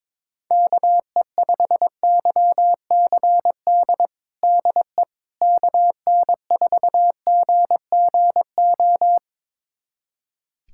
Morse code, heard as a keyed tone, about 22 wpm, 700 Hz. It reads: KE5YCD DE KN4GGO
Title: USA Callsign DE Callsign - 6 Characters - 22wpm